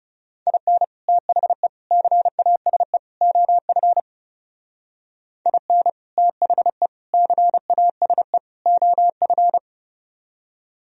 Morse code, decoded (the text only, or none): IN THE CASE OF IN THE CASE OF